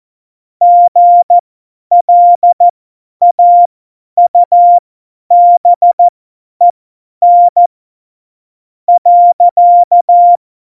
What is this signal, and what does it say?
Morse code, heard as a keyed tone, about 14 wpm, spaced as for 9 wpm, 700 Hz. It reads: GLAUBEN.
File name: Morsekurs / Witze / Witz 20. Oktober 2020, 50 F70